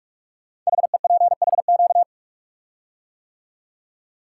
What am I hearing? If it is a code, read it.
Morse code, heard as a keyed tone, about 44 wpm, 700 Hz. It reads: HEPHQ
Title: Code Groups - 5 Letters - 45wpm